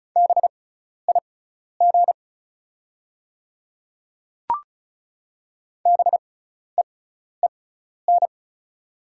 Morse code, heard as a keyed tone, about 36 wpm, spaced as for 12 wpm, 690 Hz. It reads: BIG BEEN
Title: 6x Character Spacing - Top 200 Words - 35wpm